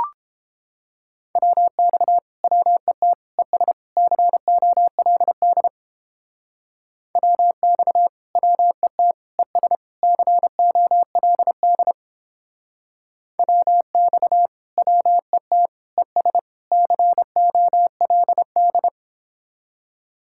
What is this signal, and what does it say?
Morse code, heard as a keyed tone, about 29 wpm, 700 Hz. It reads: WX WET ES COLD WX WET ES COLD WX WET ES COLD